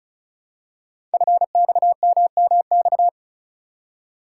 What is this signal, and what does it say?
Morse code, heard as a keyed tone, about 36 wpm, 690 Hz. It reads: FXMMX
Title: Code Groups - 5 Letters - 35wpm